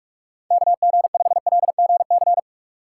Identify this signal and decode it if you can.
Morse code, heard as a keyed tone, about 47 wpm, 700 Hz. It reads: KG5LGC